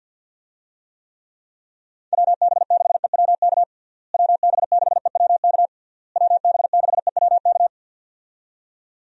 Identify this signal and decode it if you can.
Morse code, heard as a keyed tone, about 52 wpm, 700 Hz. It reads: WB6EWK WB6EWK WB6EWK
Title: USA Call Signs - 6 Characters - Repeated 3x - 50wpm